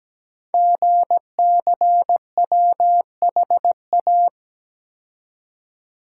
Morse code, heard as a keyed tone, about 17 wpm, 700 Hz. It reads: GCWHA
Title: Code Groups - 5 Letter:Numbers - 17wpm